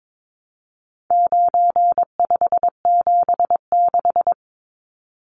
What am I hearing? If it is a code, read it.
Morse code, heard as a keyed tone, about 22 wpm, 700 Hz. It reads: 9576
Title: Numbers - 4-Digit 22wpm